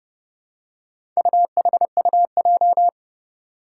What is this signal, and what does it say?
Morse code, heard as a keyed tone, about 30 wpm, 700 Hz. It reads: UHUJ
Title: Four Challenging Letter-Number Combinations 30wpm